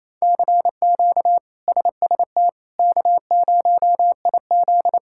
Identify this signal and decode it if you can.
Morse code, heard as a keyed tone, about 27 wpm, 700 Hz. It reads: CQ SST K0IZ
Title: K1USN SST QSO Practice - 28wpm